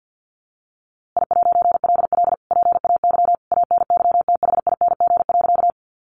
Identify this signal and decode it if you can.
Morse code, heard as a keyed tone, about 50 wpm, 700 Hz. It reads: I'LL PAY ANYTHING.
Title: A Sound Of Thunder 50wpm